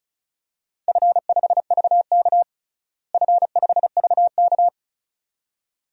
Morse code, heard as a keyed tone, about 34 wpm, 700 Hz. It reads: F5VK F5VK
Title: DXCC-100 Call Signs - 4 Characters - Repeated 2x - 35wpm